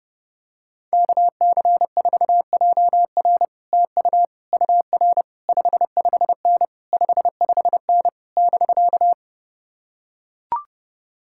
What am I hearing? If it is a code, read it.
Morse code, heard as a keyed tone, about 32 wpm, 700 Hz. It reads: KC4JR TU UR 55N 55N <BK>